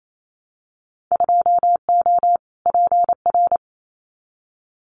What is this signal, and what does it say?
Morse code, heard as a keyed tone, about 28 wpm, 700 Hz. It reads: 2O PR
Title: Winter Field Day Exchanges Encore 28wpm